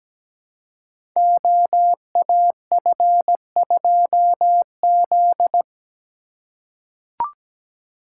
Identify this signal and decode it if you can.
Morse code, heard as a keyed tone, about 17 wpm, 700 Hz. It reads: OAF2Z
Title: Code Groups - 5 Letter:Numbers - 17wpm